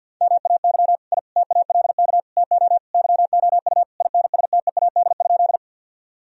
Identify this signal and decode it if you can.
Morse code, heard as a keyed tone, about 50 wpm, 700 Hz. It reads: MAY I TALK TO YOU INSTEAD?
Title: Sentences from Top 1000 Words - Review Part 4 - 50wpm